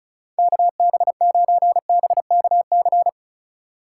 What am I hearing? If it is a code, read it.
Morse code, heard as a keyed tone, about 36 wpm, 700 Hz. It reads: KB9BKC
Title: USA Call Signs - 6 Characters - 4x Character Spacing - 35wpm